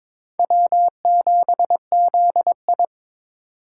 Morse code, heard as a keyed tone, about 22 wpm, 700 Hz. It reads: W7ZI
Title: USA Call Signs - 4 Characters - 2x Character Spacing - 22wpm